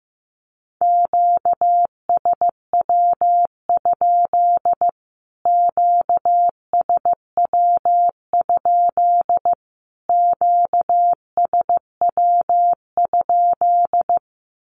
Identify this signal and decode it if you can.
Morse code, heard as a keyed tone, about 15 wpm, 700 Hz. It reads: QSW? QSW? QSW?
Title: Q-Codes Repeated 3x 15wpm